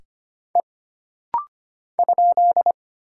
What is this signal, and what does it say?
Morse code, heard as a keyed tone, about 25 wpm, 700 Hz. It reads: E ?
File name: Single Character (Letter:Number:Punctuation:Prosign) - Rapid-Fire - 25wpm